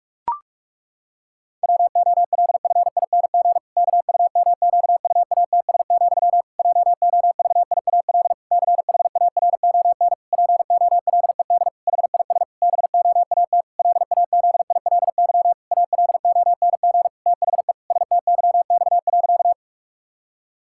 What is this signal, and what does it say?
Morse code, heard as a keyed tone, about 46 wpm, 700 Hz. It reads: WOLFING KUMQUATS, JOVIAL CHARON POLED HIS BOAT LAZILY ALONG THE STYX.
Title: Pangram Challenge v2 - 45wpm